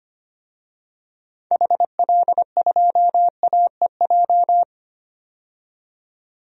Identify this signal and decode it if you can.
Morse code, heard as a keyed tone, about 25 wpm, 700 Hz. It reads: HL2AEJ
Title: DXCC-100 Call Signs - 6 Characters - 25wpm